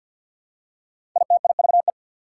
Morse code, heard as a keyed tone, about 48 wpm, 700 Hz. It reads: ITIVE
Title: Prefixes and Suffixes 50wpm